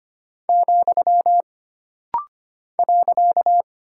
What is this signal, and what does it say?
Morse code, heard as a keyed tone, about 25 wpm, 700 Hz. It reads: , .
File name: Single Punctuation-Prosign - Rapid-Fire - 25wpm